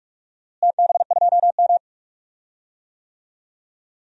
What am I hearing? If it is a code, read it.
Morse code, heard as a keyed tone, about 46 wpm, 700 Hz. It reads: TB2M
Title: DXCC-100 Call Signs - 4 Characters - Repeated 2x - 45wpm